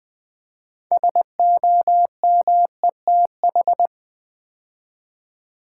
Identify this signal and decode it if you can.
Morse code, heard as a keyed tone, about 20 wpm, 700 Hz. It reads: SOMETH
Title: Build A Word - 3 Letters At A Time - 9-Letter Words - 20wpm